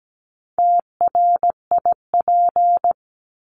Morse code, heard as a keyed tone, about 17 wpm, 700 Hz. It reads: TRIP
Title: Top 1251-1500 Words 17wpm